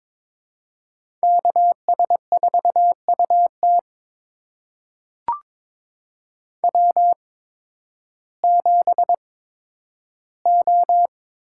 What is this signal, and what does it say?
Morse code, heard as a keyed tone, about 22 wpm, spaced as for 12 wpm, 700 Hz. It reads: KS4UT W7O
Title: USA Call Signs - 8x Character Spacing - 22wpm